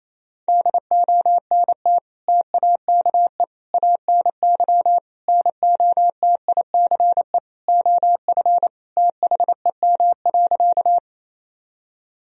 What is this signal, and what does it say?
Morse code, heard as a keyed tone, about 27 wpm, 700 Hz. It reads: DONT TAKE ANY NOTICE OF THEM.
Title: Sentences from Top 1000 Words - Review Part 2 - 28wpm